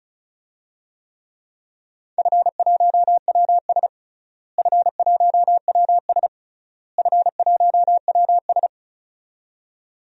Morse code, heard as a keyed tone, about 34 wpm, 700 Hz. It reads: F1WS F1WS F1WS
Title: DXCC-100 Call Signs - 4 Characters - Repeated 3x - 35wpm